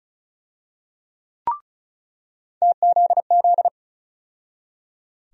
Morse code, heard as a keyed tone, about 36 wpm, 700 Hz. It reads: TZZ
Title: USA Call Sign Postfixes - 3 Letters 35wpm